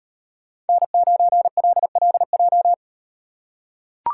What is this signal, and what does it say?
Morse code, heard as a keyed tone, about 38 wpm, 700 Hz. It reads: N9LLJ